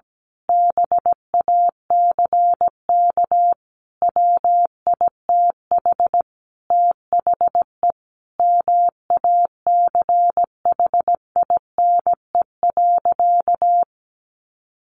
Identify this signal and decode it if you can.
Morse code, heard as a keyed tone, about 17 wpm, 700 Hz. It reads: BACK WITH THE MACHINE.
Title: A Sound Of Thunder 17wpm